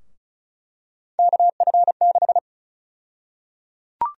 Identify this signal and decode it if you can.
Morse code, heard as a keyed tone, about 34 wpm, 700 Hz. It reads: KF6
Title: USA Call Sign Prefixes - 35wpm